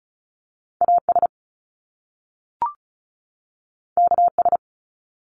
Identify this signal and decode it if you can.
Morse code, heard as a keyed tone, about 36 wpm, 700 Hz. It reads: AS KS
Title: Top Two Letter Combinations in Words 35wpm